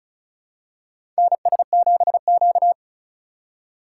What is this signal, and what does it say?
Morse code, heard as a keyed tone, about 34 wpm, 700 Hz. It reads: NS7Q
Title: USA Call Signs - 4 Characters - 2x Character Spacing - 35wpm